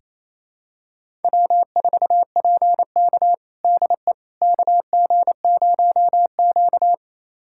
Morse code, heard as a keyed tone, about 28 wpm, 700 Hz. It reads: W4PK DE KG0Q
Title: USA Callsign DE Callsign - 4 Characters - 28wpm